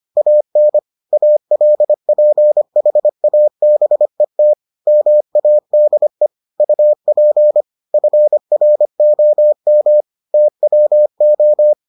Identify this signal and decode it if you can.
Morse code, heard as a keyed tone, about 25 wpm, 600 Hz. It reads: AN ALPHABET MADE UP FROM TWO